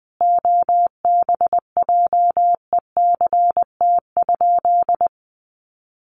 Morse code, heard as a keyed tone, about 20 wpm, 700 Hz. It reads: OBJECT?
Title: Sentences from Top 500 Words 20wpm